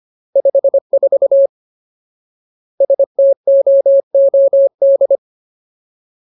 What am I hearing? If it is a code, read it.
Morse code, heard as a keyed tone, about 25 wpm, 550 Hz. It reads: 54 STOOD
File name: Ultimate Word List / 25WPM Ultimate